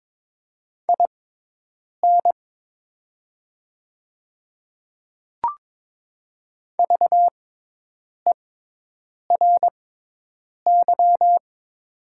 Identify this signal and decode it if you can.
Morse code, heard as a keyed tone, about 22 wpm, spaced as for 8 wpm, 700 Hz. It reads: IN VERY
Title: 6x Character Spacing - Top 300 Words - 22wpm